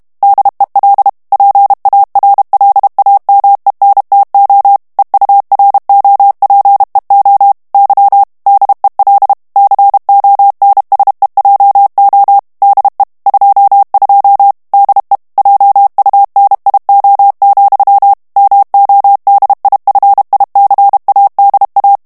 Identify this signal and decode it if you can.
Morse code, heard as a keyed tone, about 32 wpm, 800 Hz. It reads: DEL PARLAMENTO EUROPEO Y DEL CONSEJO DE 22 DE JUNIO, MODIFICADA
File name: ESPAGNOL TEXTE EN CLAIR 2 VITESSE 30 MOTS MINUTES